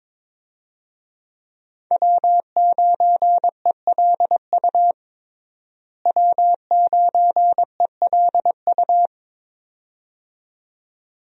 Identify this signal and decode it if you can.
Morse code, heard as a keyed tone, about 22 wpm, 700 Hz. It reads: W9ELU W9ELU
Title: USA Call Signs - 5 Characters - Repeated 2x - 22wpm